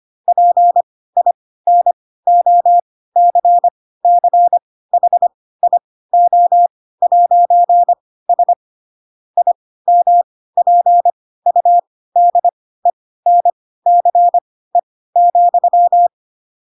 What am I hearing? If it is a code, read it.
Morse code, heard as a keyed tone, about 25 wpm, spaced as for 16 wpm, 700 Hz. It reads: PINOCCHIO'S IMPUDENCE,
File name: Pinocchio / Pinocchio ch4 10 25 ik1hge